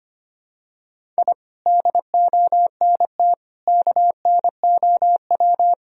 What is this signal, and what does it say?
Morse code, heard as a keyed tone, about 25 wpm, 700 Hz. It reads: I DONT KNOW
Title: Sentences from Top 1000 Words - Review Part 1 - 25wpm